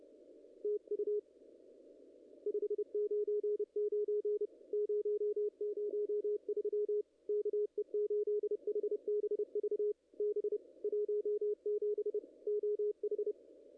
Morse code, heard as a keyed tone, about 28 wpm, 410 Hz, with radio noise. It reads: TU 599 003 KE8HBV B 17 OH
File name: contests / ss-cw-k5tr-op-n5rz -0000